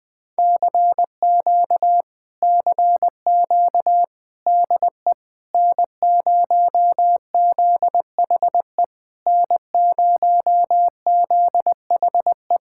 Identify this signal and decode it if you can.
Morse code, heard as a keyed tone, about 20 wpm, 700 Hz. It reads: CQ CQ DE N0ZHE N0ZHE
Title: Speed-Racing Common QSO Phrases - 17wpm